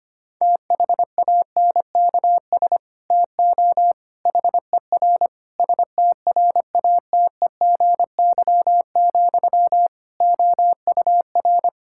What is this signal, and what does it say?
Morse code, heard as a keyed tone, about 25 wpm, 700 Hz. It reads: THANKS TO HER STRATEGY, OUR
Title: Sentences from Top 1000 Words - Review Part 5 - 25wpm